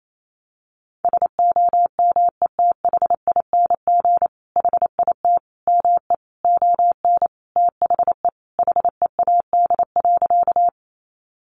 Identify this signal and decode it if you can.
Morse code, heard as a keyed tone, about 28 wpm, 700 Hz. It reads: SOMETHING HIT ME ON THE HEAD.
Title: Sentences from Top 1200 Words 28wpm